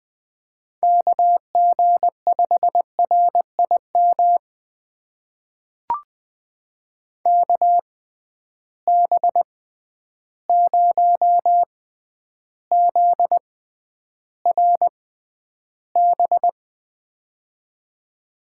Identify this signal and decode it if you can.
Morse code, heard as a keyed tone, about 20 wpm, spaced as for 10 wpm, 700 Hz. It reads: KG5RIM KB0ZRB
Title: USA Call Signs - 6 Characters - 6x Character Spacing - 20wpm